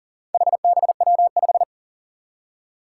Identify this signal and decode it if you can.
Morse code, heard as a keyed tone, about 40 wpm, 700 Hz. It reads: HBW5
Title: Four Challenging Letter-Number Combinations 40wpm